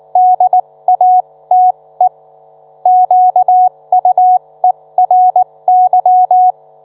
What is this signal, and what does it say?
Morse code, heard as a keyed tone, about 19 wpm, spaced as for 16 wpm, 730 Hz. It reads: DATE QUERY